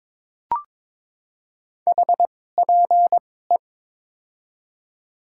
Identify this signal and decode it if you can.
Morse code, heard as a keyed tone, about 22 wpm, spaced as for 16 wpm, 700 Hz. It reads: HPE
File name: Top 90 QSO Elements - 2x Character Spacing - 22wpm